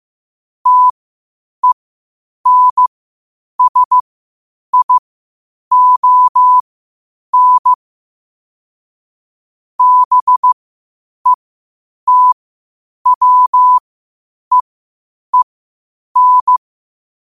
Morse code, heard as a keyed tone, about 15 wpm, spaced as for 8 wpm, 1000 Hz. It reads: TENSION BETWEEN
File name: qotdfiles / 8wpm